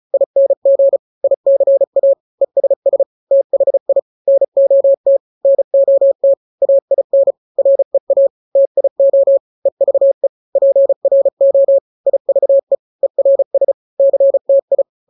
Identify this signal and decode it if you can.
Morse code, heard as a keyed tone, about 34 wpm, 550 Hz. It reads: ING ICA ESS THI NOT NOT AIN REA TIO EVE PRO IVE ERS CTI
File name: 3-grams 35 30 WPM